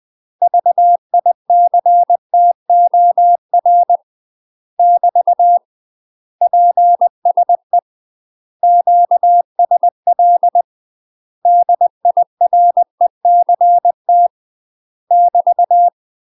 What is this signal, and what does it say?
Morse code, heard as a keyed tone, about 20 wpm, 700 Hz. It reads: VICTOR <BT> PSE QSL DIRECT <BT>
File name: UK MASTER QSOs TRAINING / 20WPM / UK MASTER QSOs 0055